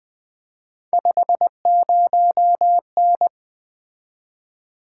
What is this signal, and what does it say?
Morse code, heard as a keyed tone, about 20 wpm, 700 Hz. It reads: 50N
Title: Three Letter-Number Combinations - 20wpm